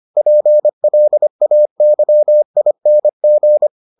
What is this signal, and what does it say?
Morse code, heard as a keyed tone, about 25 wpm, 600 Hz. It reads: PLAYING